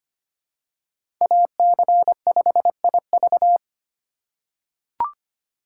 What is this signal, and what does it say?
Morse code, heard as a keyed tone, about 25 wpm, 700 Hz. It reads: AC5IV
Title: USA Call Signs - 3x Character Spacing - 25wpm